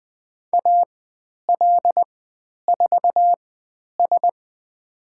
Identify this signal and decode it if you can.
Morse code, heard as a keyed tone, about 20 wpm, spaced as for 10 wpm, 700 Hz. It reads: AL4S
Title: 102 Lesson 17g - Letter u - Call Signs 20wpm